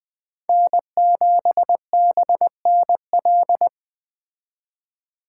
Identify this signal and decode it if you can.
Morse code, heard as a keyed tone, about 20 wpm, 700 Hz. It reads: N7BNL